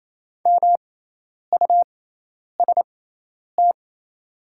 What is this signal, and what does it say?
Morse code, heard as a keyed tone, about 27 wpm, spaced as for 10 wpm, 700 Hz. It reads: MUST
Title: 6x Character Spacing - Top 200 Words - 28wpm